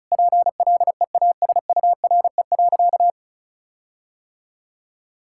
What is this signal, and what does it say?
Morse code, heard as a keyed tone, about 36 wpm, 700 Hz. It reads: PLEASURE.
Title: Sentences from Top 1600 Words 35wpm